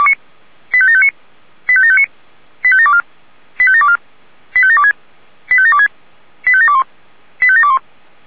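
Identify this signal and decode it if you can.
Five-tone selective call (ZVEI), heard as quick runs of five tones, tones of about 70 ms. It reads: 85759 85759 85724 85724 85726 85726 85721 85721
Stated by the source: ZVEI five-tone call received on an SDR.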